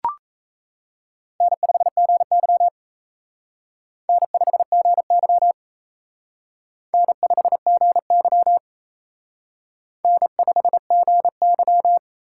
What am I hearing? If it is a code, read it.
Morse code, heard as a keyed tone, about 33 wpm, 700 Hz. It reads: N5GY N5GY N5GY N5GY
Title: Speed Racing - USA 1x2 Call Signs 28wpm